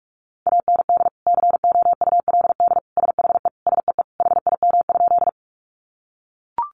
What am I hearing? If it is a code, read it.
Morse code, heard as a keyed tone, about 44 wpm, 700 Hz. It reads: AND COULD SHE SEE HIM?
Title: Sentences from Top 100 Words 45wpm